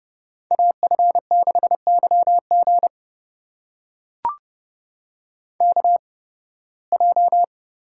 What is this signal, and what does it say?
Morse code, heard as a keyed tone, about 30 wpm, spaced as for 20 wpm, 700 Hz. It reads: AF6YG KJ
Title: USA Call Signs - 8x Character Spacing - 30wpm